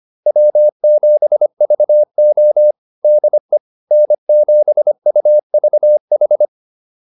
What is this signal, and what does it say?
Morse code, heard as a keyed tone, about 25 wpm, 600 Hz. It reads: W7VO DE N7UVH
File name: QSO105 25